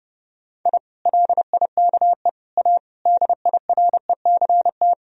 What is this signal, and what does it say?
Morse code, heard as a keyed tone, about 30 wpm, 700 Hz. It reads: I LIKE A DIRECT
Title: Sentences from Top 1000 Words - Review Part 2 - 30wpm